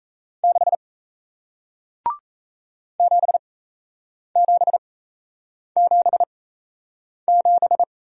Morse code, heard as a keyed tone, about 36 wpm, spaced as for 8 wpm, 700 Hz. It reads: 6 7777